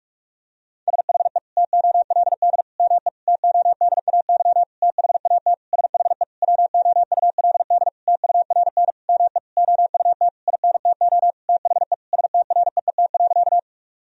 Morse code, heard as a keyed tone, about 46 wpm, 700 Hz. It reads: SHE TOLD ME TODAY THAT SHE WOULD TURN ME OUT INTO THE STREET.
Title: Sentences from Top 1000 Words - Review Part 4 - 45wpm